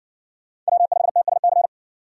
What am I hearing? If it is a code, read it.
Morse code, heard as a keyed tone, about 53 wpm, 700 Hz. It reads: W5TSC